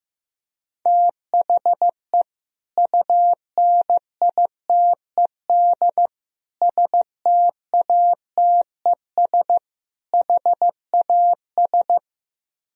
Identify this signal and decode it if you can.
Morse code, heard as a keyed tone, about 15 wpm, 700 Hz. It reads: THE UNITED STATES HAS
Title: Speed-Racing - Sets of 4 Words 15wpm